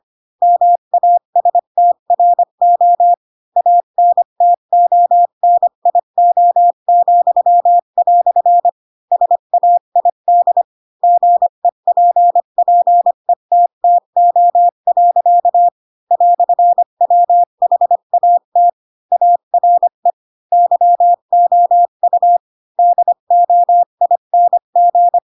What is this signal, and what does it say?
Morse code, heard as a keyed tone, about 25 wpm, 700 Hz. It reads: MASTRO ANTONIO," SAID GEPPETTO. "WHAT ARE YOU DOING